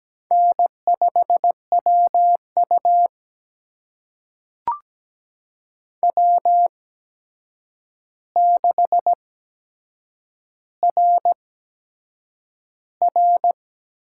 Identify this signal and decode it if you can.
Morse code, heard as a keyed tone, about 17 wpm, spaced as for 7 wpm, 700 Hz. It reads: N5WU W6RR
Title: USA Call Signs - 4 Characters - 8x Character Spacing - 17wpm